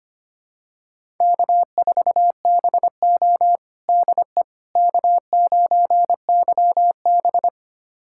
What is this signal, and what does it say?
Morse code, heard as a keyed tone, about 25 wpm, 700 Hz. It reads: K4BO DE K9YB
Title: USA Callsign DE Callsign - 4 Characters - 25wpm